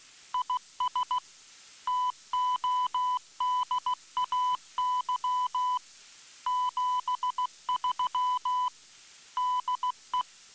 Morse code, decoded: IS TODAY 73 DE